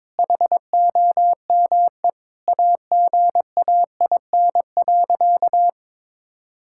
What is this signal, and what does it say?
Morse code, heard as a keyed tone, about 22 wpm, 700 Hz. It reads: HOME AGAIN.
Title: Sentences from Top 1000 Words - Review Part 6 - 22wpm